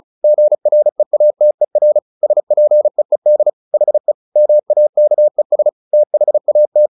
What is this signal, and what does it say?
Morse code, heard as a keyed tone, about 34 wpm, 600 Hz. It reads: GREATER SPEED HE MAKES THAT
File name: the old man and the sea 2 35wpm